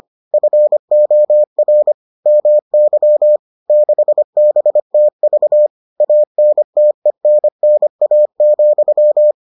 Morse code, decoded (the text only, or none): FOR MY 6BTV ANTENNA,